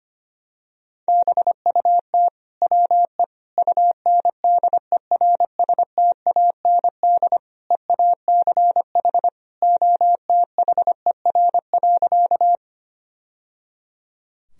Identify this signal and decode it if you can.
Morse code, heard as a keyed tone, about 25 wpm, 700 Hz. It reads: BUT WE UNDERSTAND EACH OTHER.
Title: Sentences from Top 1000 Words - Review Part 1 - 25wpm